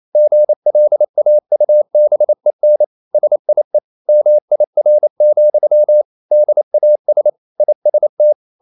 Morse code, heard as a keyed tone, about 28 wpm, 600 Hz. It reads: GLAUBEN SIE MIR, DAS IST
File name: Liebespaar4 28wpm